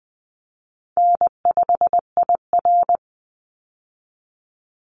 Morse code, heard as a keyed tone, about 20 wpm, 700 Hz. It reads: N5IR
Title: Easy Call Signs Repeated 2x - 20wpm